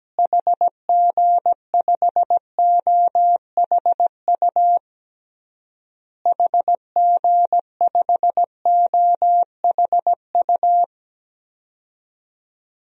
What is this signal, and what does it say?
Morse code, heard as a keyed tone, about 17 wpm, 700 Hz. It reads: HG5OHU HG5OHU
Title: DXCC-100 Call Signs - 6 Characters - Repeated 3x - 17wpm